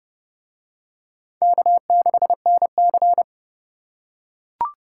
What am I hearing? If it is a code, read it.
Morse code, heard as a keyed tone, about 28 wpm, 700 Hz. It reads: K6NC